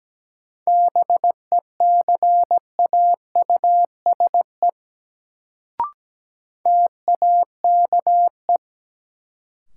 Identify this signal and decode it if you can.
Morse code, heard as a keyed tone, about 17 wpm, 700 Hz. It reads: BECAUSE TAKE
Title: Sets of 2 Words - New Word Drill - 17wpm